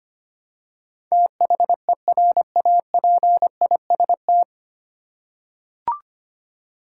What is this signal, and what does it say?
Morse code, heard as a keyed tone, about 25 wpm, 700 Hz. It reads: THERAPIST
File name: Nine Letter Words - 25wpm